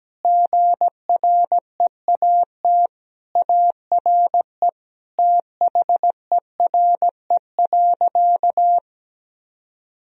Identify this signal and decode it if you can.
Morse code, heard as a keyed tone, about 17 wpm, 700 Hz. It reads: GREAT ARE THERE.
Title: Sentences from Top 1000 Words - Review Part 1 - 17wpm